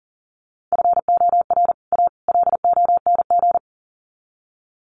Morse code, heard as a keyed tone, about 40 wpm, 700 Hz. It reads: FOR A LONG
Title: Sets of 3 Words 40wpm